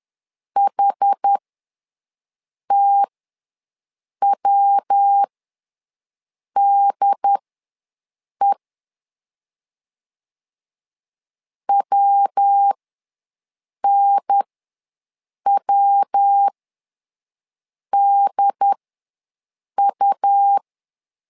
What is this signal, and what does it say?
Morse code, heard as a keyed tone, about 10 wpm, spaced as for 5 wpm, 800 Hz. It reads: HTWDE WNWDU
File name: LEZIONI / 3 MWRO / 3 MW ws 10wpm 2213